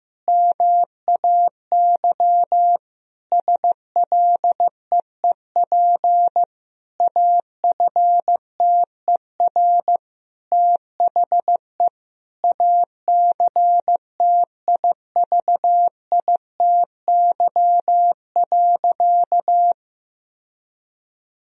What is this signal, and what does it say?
Morse code, heard as a keyed tone, about 15 wpm, 700 Hz. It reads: MAY SLEEP AFTER THE ACTIVITY.
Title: Sentences from Top 1000 Words - Review Part 4 - 15wpm